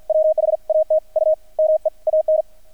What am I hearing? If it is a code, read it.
Morse code, heard as a keyed tone, about 50 wpm, spaced as for 43 wpm, 630 Hz. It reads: WHAT U MEAN
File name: archived / CRSnet